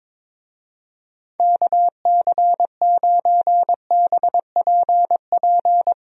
Morse code, heard as a keyed tone, about 22 wpm, 700 Hz. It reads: KC9BPP
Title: USA Call Signs - 6 Characters - 22wpm